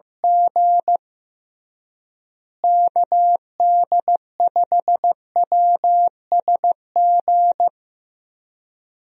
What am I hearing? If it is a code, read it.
Morse code, heard as a keyed tone, about 15 wpm, 700 Hz. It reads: G KD5WSG
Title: USA Call Signs - 6 Characters - Repeated 3x - 15wpm